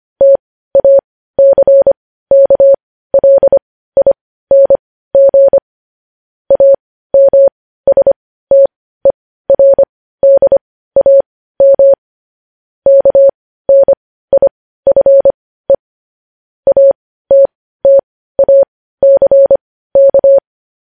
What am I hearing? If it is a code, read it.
Morse code, heard as a keyed tone, about 25 wpm, spaced as for 15 wpm, 550 Hz. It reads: TACKLING AMSTERDAM KNIFE ATTACK
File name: podcast / News headlines-1x-25x15